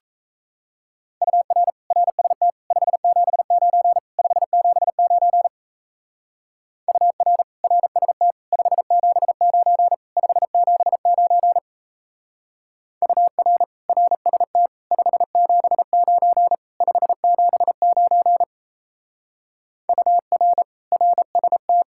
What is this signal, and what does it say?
Morse code, heard as a keyed tone, about 33 wpm, 700 Hz. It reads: UR RST 579 579 UR RST 579 579 UR RST 579 579 UR RST
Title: Speed-Racing Common QSO Phrases - 28wpm